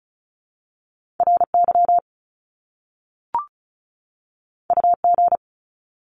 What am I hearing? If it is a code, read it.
Morse code, heard as a keyed tone, about 36 wpm, 700 Hz. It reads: RY UG
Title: Top Two Letter Combinations in Words 35wpm